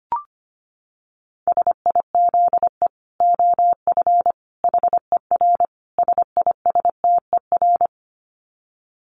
Morse code, heard as a keyed tone, about 25 wpm, 700 Hz. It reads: SIZE OF HER SISTER
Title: Dit Challenge 25wpm